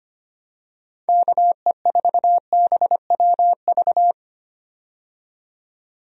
Morse code, heard as a keyed tone, about 25 wpm, 700 Hz. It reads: KE4BWV
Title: USA Call Signs - 6 Characters - 25wpm